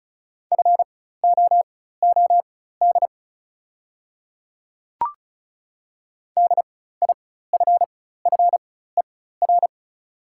Morse code, heard as a keyed tone, about 36 wpm, spaced as for 16 wpm, 690 Hz. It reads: FOOD DIFFER
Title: 4x Character Spacing - Top 500 Words - 35wpm